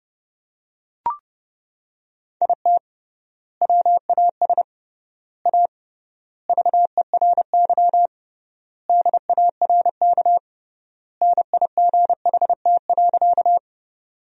Morse code, heard as a keyed tone, about 30 wpm, 700 Hz. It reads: IT WAS A VERY DARK NIGHT.
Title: 3x Word-Spacing - Sentences from Top 500 Words - 30wpm